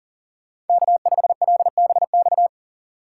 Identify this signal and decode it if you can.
Morse code, heard as a keyed tone, about 50 wpm, 700 Hz. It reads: K5LBX
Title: Speed Racing - USA 1x3 Call Signs 40wpm